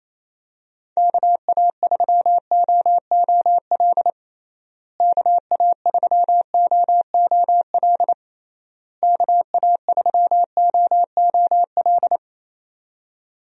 Morse code, heard as a keyed tone, about 27 wpm, 700 Hz. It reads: KA3OOL KA3OOL KA3OOL
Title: USA Call Signs - 6 Characters - Repeated 3x - 28wpm